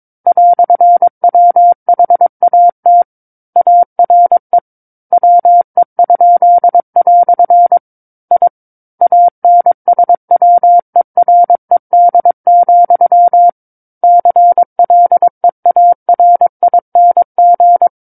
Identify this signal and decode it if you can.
Morse code, heard as a keyed tone, about 22 wpm, 700 Hz. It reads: "WHAT ARE WE?" I ANSWERED, CLEARING
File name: Book One / 13-Chapter 13 - 22 WPM